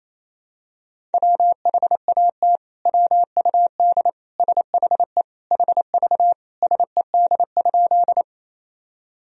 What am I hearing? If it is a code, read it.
Morse code, heard as a keyed tone, about 28 wpm, 700 Hz. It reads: WHAT WUD SHE HV SED?
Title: Sentences from Top 100 Words and Abbreviations 28wpm